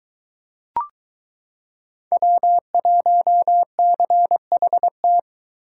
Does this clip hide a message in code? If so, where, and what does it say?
Yes: Morse code from 2.1 to 5.2 s, heard as a keyed tone, about 23 wpm, 700 Hz. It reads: W1CHT